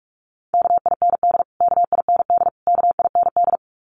Morse code, heard as a keyed tone, about 44 wpm, 700 Hz. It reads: KIND KIND KIND
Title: Top 200 Words Repeated 45wpm